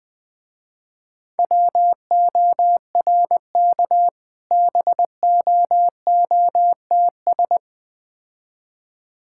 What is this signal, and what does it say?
Morse code, heard as a keyed tone, about 20 wpm, 700 Hz. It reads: WORK BOOTS
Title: Difficult Words - Speed-Racing with Hints - 20wpm